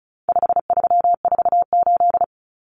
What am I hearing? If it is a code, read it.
Morse code, heard as a keyed tone, about 34 wpm, 700 Hz. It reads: 5348
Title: Common Cut Numbers - 4-Digit 35wpm